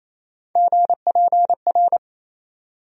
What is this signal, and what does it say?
Morse code, heard as a keyed tone, about 28 wpm, 700 Hz. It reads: GPR
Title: USA Call Sign Postfixes - 3 Letters 28wpm